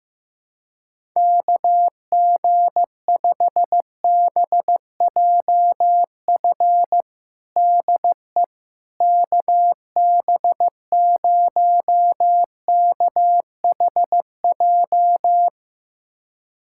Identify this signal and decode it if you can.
Morse code, heard as a keyed tone, about 15 wpm, 700 Hz. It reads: KG5BJF DE KB0KHJ
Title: USA Callsign DE Callsign - 6 Characters - 15wpm